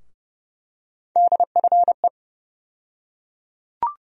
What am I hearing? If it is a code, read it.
Morse code, heard as a keyed tone, about 30 wpm, 700 Hz. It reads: DFE
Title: USA Call Sign Postfixes - 3 Letters 30wpm